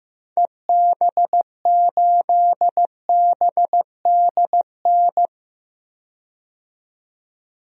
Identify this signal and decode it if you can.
Morse code, heard as a keyed tone, about 16 wpm, 700 Hz. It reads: EB8BDN
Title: DXCC-100 Call Signs - 6 Characters - 15wpm